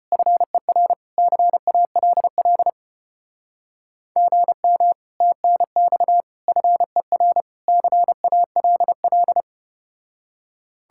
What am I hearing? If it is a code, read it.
Morse code, heard as a keyed tone, about 33 wpm, 700 Hz. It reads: FER CALL GM TNX FER CALL